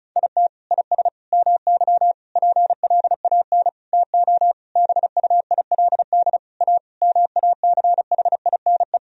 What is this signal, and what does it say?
Morse code, heard as a keyed tone, about 34 wpm, 700 Hz. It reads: IT IS MY PLAN TO BUILD A MACHINE
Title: Sentences from Top 1000 Words - Review Part 2 - 35wpm